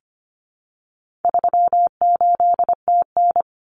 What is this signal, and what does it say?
Morse code, heard as a keyed tone, about 25 wpm, 700 Hz. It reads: 38TN
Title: Common Cut Numbers - 4-Digit 25wpm